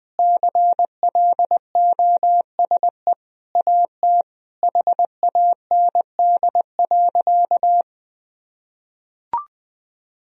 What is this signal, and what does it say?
Morse code, heard as a keyed tone, about 21 wpm, 700 Hz. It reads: CLOSE AT HAND.
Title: Sentences from Top 1000 Words - Review Part 1 - 20wpm